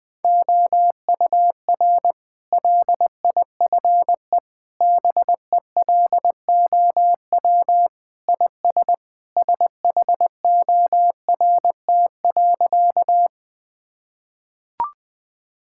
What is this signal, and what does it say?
Morse code, heard as a keyed tone, about 20 wpm, 700 Hz. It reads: OUR LIFE BELOW IS SHORT.
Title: Sentences from Top 1000 Words - Review Part 5 - 20wpm